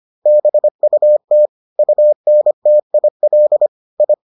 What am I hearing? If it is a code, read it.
Morse code, heard as a keyed tone, about 25 wpm, 600 Hz. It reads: BUT UNTIL I